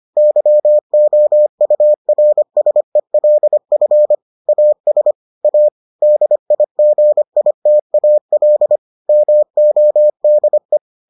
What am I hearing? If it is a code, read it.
Morse code, heard as a keyed tone, about 25 wpm, 600 Hz. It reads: YOURSELF AS A DIGITAL MODE